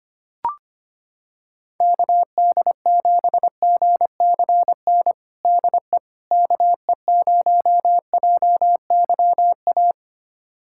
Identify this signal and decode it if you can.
Morse code, heard as a keyed tone, about 25 wpm, 700 Hz. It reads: KD7GCN DE KE0JYA
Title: USA Callsign DE Callsign 25wpm